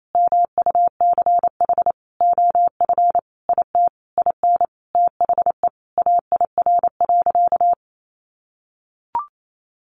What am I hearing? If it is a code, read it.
Morse code, heard as a keyed tone, about 28 wpm, 690 Hz. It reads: MUCH OF IT IN THE AIR.
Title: Sentences from Top 900 Words 28wpm